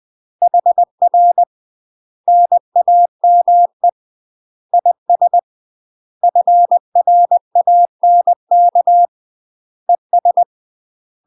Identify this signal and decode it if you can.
Morse code, heard as a keyed tone, about 20 wpm, 700 Hz. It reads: HR NAME IS FRANK ES
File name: UK MASTER QSOs TRAINING / 20WPM / UK MASTER QSOs 0014